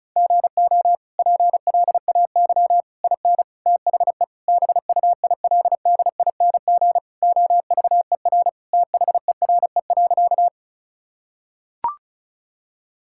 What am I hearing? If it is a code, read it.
Morse code, heard as a keyed tone, about 36 wpm, 700 Hz. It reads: GO PLAY IN THE BUILDING OVER THERE.
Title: Sentences from Top 1000 Words - Review Part 5 - 35wpm